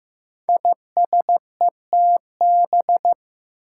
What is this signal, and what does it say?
Morse code, heard as a keyed tone, about 15 wpm, 700 Hz. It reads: ISETB
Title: Five Challenging Letter-Number Combinations 15wpm